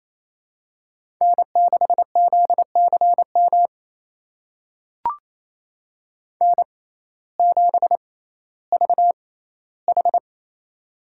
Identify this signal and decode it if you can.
Morse code, heard as a keyed tone, about 28 wpm, spaced as for 15 wpm, 700 Hz. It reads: N6ZCM N7VH
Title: USA Call Signs - 6x Character Spacing - 28wpm